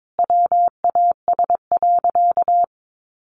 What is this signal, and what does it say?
Morse code, heard as a keyed tone, about 22 wpm, 700 Hz. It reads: WAS.
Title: Sentences from Top 1000 Words 22wpm